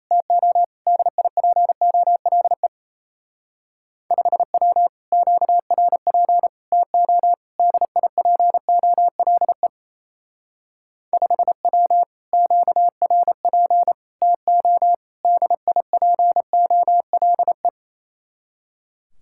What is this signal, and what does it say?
Morse code, heard as a keyed tone, about 32 wpm, 700 Hz. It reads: TO DIPOLE 5W QRP TO DIPOLE 5W QRP TO DIPOLE
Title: Speed-Racing Common QSO Phrases Encore - 28wpm